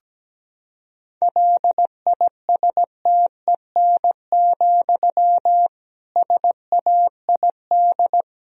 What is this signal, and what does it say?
Morse code, heard as a keyed tone, about 17 wpm, 700 Hz. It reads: LISTEN, SAID
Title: Sentences from Top 1000 Words - Review Part 3 - 17wpm